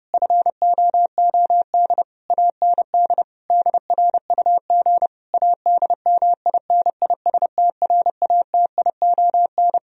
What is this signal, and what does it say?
Morse code, heard as a keyed tone, about 31 wpm, 700 Hz. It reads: FOOD AND DRUG ADMINISTRATION